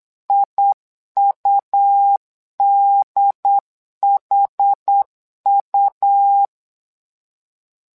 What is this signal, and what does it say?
Morse code, heard as a keyed tone, about 8 wpm, 800 Hz. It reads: IUDHU
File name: LEZIONI / 2 UHDN / 2 UHD 8wpm 2258